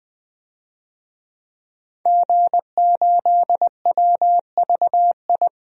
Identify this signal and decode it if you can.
Morse code, heard as a keyed tone, about 20 wpm, 700 Hz. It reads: G8WVI
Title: DXCC-100 Call Signs - 5 Characters - 20wpm